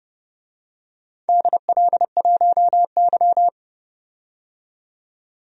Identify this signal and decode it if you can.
Morse code, heard as a keyed tone, about 30 wpm, 700 Hz. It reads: DL1Y
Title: DXCC-100 Call Signs - 4 Characters - 30wpm